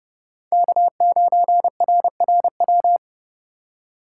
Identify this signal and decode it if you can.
Morse code, heard as a keyed tone, about 30 wpm, 700 Hz. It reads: K9RRW
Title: USA Call Signs - 2x Character Spacing - 30wpm